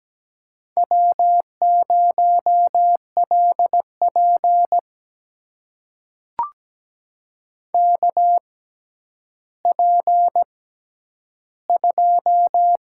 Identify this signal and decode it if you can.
Morse code, heard as a keyed tone, about 17 wpm, spaced as for 10 wpm, 700 Hz. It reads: W0LP KP2
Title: USA Call Signs - 4 Characters - 6x Character Spacing - 17wpm